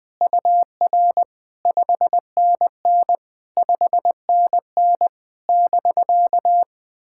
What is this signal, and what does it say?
Morse code, heard as a keyed tone, about 20 wpm, 700 Hz. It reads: UR 5NN 5NN <BK>
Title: POTA QSO Practice - 20wpm